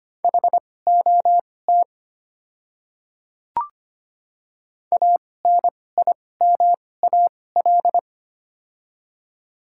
Morse code, heard as a keyed tone, about 25 wpm, spaced as for 18 wpm, 700 Hz. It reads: HOT ANIMAL
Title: 2x Character Spacing - Top 200 Words - 25wpm